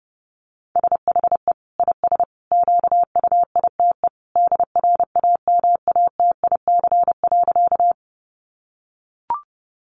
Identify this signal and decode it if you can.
Morse code, heard as a keyed tone, about 30 wpm, 700 Hz. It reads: SHE IS QUITE DRAMATIC.
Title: Sentences from Top 1300 Words 30wpm